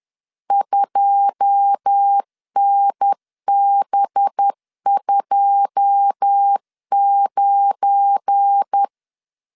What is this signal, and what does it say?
Morse code, heard as a keyed tone, about 10 wpm, 800 Hz. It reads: 2NB29
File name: LEZIONI / 9 2 10wpm 2736